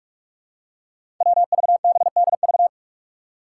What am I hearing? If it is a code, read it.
Morse code, heard as a keyed tone, about 41 wpm, 700 Hz. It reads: WVBDV
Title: Five Challenging Letter-Number Combinations 45wpm